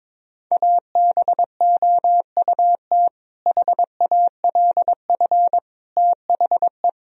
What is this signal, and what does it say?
Morse code, heard as a keyed tone, about 22 wpm, 700 Hz. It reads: ABOUT HALF THE
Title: Sentences from Top 1000 Words - Review Part 1 - 22wpm